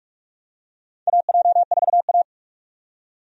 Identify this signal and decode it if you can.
Morse code, heard as a keyed tone, about 47 wpm, 700 Hz. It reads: AJ4A
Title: USA Call Signs - 4 Characters - 45wpm